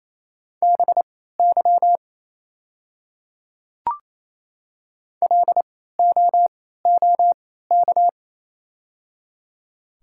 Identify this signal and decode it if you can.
Morse code, heard as a keyed tone, about 28 wpm, spaced as for 16 wpm, 700 Hz. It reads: BY LOOK
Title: 3x Character Spacing - Top 100 Words - 28wpm